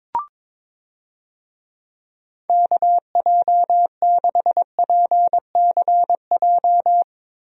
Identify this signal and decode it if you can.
Morse code, heard as a keyed tone, about 22 wpm, 700 Hz. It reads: KJ6PCJ
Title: USA Call Signs - 6 Characters - Repeated 2x - 22wpm